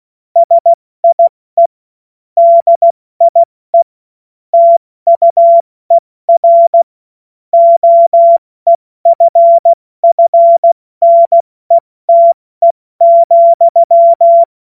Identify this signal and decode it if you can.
Morse code, heard as a keyed tone, about 16 wpm, 680 Hz. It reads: SIE DIE TUER OEFFNETE,